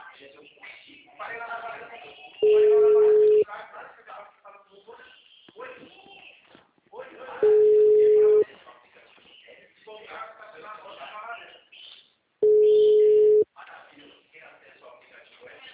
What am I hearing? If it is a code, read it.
Ringback tone (Europe, 425 Hz), with speech.